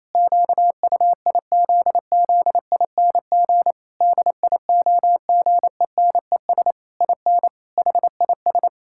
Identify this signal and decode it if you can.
Morse code, heard as a keyed tone, about 27 wpm, 700 Hz. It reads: QUIZZING DIOGENES IN HIS
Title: Pangram Challenge v2 - 28wpm